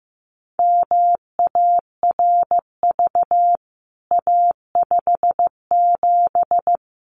Morse code, heard as a keyed tone, about 15 wpm, 700 Hz. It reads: MARV A57
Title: CWT Exchange 15wpm